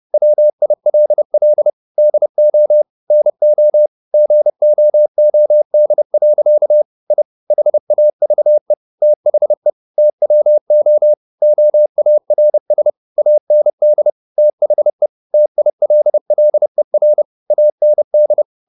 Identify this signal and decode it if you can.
Morse code, heard as a keyed tone, about 30 wpm, 600 Hz. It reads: WILL DO NO GOOD. I HAVE THE TWO OARS AND THE TILLER AND